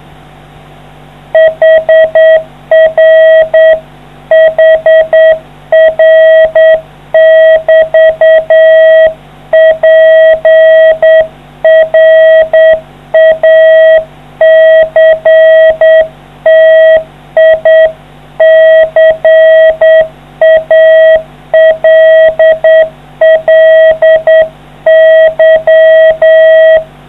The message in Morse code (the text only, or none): HR HR<BT> PRACTICALLY